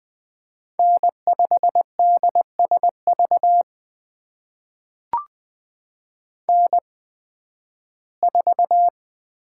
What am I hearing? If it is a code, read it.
Morse code, heard as a keyed tone, about 20 wpm, spaced as for 13 wpm, 700 Hz. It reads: N5DSV N4
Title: USA Call Signs - 5 Characters - 8x Character Spacing - 20wpm